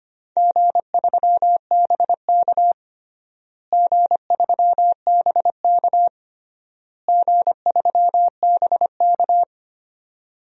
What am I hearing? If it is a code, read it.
Morse code, heard as a keyed tone, about 25 wpm, 700 Hz. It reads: G3BK G3BK G3BK
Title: DXCC-100 Call Signs - 4 Characters - Repeated 3x - 25wpm